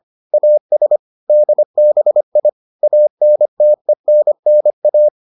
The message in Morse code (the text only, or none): AS DBI ANTENNA